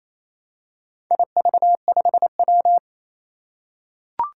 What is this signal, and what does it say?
Morse code, heard as a keyed tone, about 29 wpm, 700 Hz. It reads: IV5W
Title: Four Challenging Letter-Number Combinations 28wpm